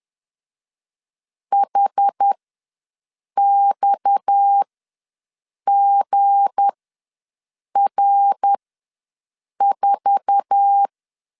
Morse code, heard as a keyed tone, about 10 wpm, spaced as for 6 wpm, 800 Hz. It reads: HXGR4
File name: LEZIONI / 10 SIGNS / 10C DOT ws 10wpm 4213